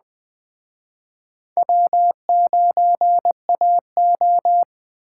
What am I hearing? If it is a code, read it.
Morse code, heard as a keyed tone, about 20 wpm, 700 Hz. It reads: W9AO